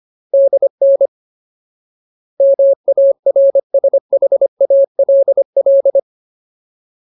Morse code, heard as a keyed tone, about 25 wpm, 550 Hz. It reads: DN MARSHALL
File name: Ultimate Word List / 25WPM Ultimate